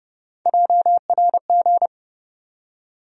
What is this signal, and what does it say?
Morse code, heard as a keyed tone, about 30 wpm, 700 Hz. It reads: JRG
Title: USA Call Sign Postfixes - 3 Letters 30wpm